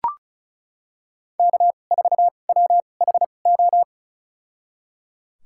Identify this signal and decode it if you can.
Morse code, heard as a keyed tone, about 36 wpm, spaced as for 25 wpm, 700 Hz. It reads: K4WHO
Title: USA Call Signs - 5 Characters - 2x Character Spacing - 35wpm